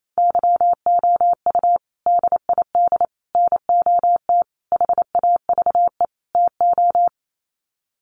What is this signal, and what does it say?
Morse code, heard as a keyed tone, about 28 wpm, 700 Hz. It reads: YOU DID NOT HAVE TO
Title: Sets of 5 Words Encore 28wpm